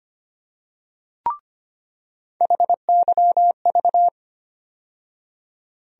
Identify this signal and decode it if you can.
Morse code, heard as a keyed tone, about 25 wpm, 700 Hz. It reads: HYV
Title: USA Call Sign Postfixes - 3 Letters 25wpm